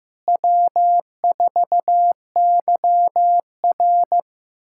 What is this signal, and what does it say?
Morse code, heard as a keyed tone, about 16 wpm, 700 Hz. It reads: W4YR
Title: USA Callsign DE Callsign - 4 Characters - 15wpm